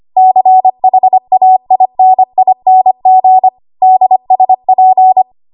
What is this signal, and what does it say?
Morse code, heard as a keyed tone, about 25 wpm, 750 Hz. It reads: CHAINING DSP
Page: W1AW Web Code Practice Files